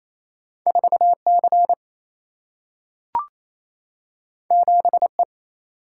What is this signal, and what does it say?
Morse code, heard as a keyed tone, about 28 wpm, 700 Hz. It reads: 4C 7E
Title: Single-Letter Single-Number 28wpm